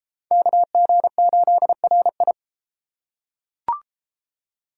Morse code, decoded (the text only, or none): KG8RI